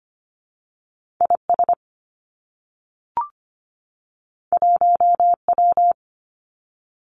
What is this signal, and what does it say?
Morse code, heard as a keyed tone, about 25 wpm, 700 Hz. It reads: I S 1 W
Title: Two Challenging Letter-Number Combinations 25wpm